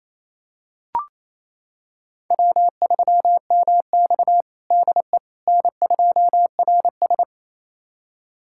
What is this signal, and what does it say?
Morse code, heard as a keyed tone, about 28 wpm, 700 Hz. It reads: W3MX DE N2RS
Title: USA Callsign DE Callsign - 4 Characters - 28wpm